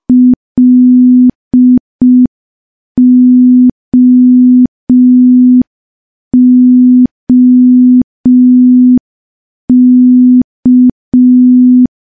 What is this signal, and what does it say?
Morse code, heard as a keyed tone, about 5 wpm, 250 Hz. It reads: LOOK
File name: morse / morse-training-05WPM